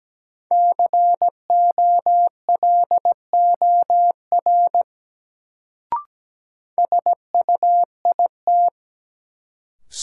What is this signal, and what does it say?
Morse code, heard as a keyed tone, about 17 wpm, 700 Hz. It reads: COLOR SUIT
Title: Sentences from Top 1600 Words - New Word Drill Alt - 17wpm